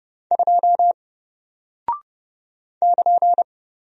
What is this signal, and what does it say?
Morse code, heard as a keyed tone, about 30 wpm, 700 Hz. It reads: 2 <KN>
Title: Single Character (Letter:Number:Punctuation:Prosign) Alt 30wpm